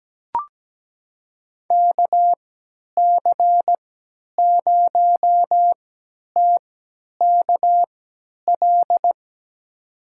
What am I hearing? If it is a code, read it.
Morse code, heard as a keyed tone, about 17 wpm, spaced as for 10 wpm, 700 Hz. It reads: KC0TKL
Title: USA Call Signs - 6 Characters - 3x Character Spacing - 17wpm